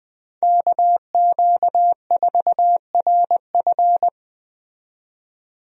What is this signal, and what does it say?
Morse code, heard as a keyed tone, about 20 wpm, 700 Hz. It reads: KQ4RF
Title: USA Call Signs - 5 Characters - 20wpm